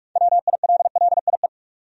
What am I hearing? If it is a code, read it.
Morse code, heard as a keyed tone, about 43 wpm, 700 Hz. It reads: WILLIE
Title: USA Common Names - 45wpm